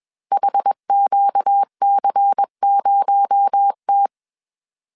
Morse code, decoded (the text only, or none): HQC0T